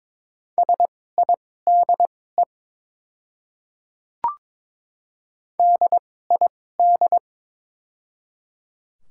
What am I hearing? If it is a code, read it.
Morse code, heard as a keyed tone, about 22 wpm, spaced as for 16 wpm, 690 Hz. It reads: SIDE DID
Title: 2x Character Spacing - Top 100 Words - 22wpm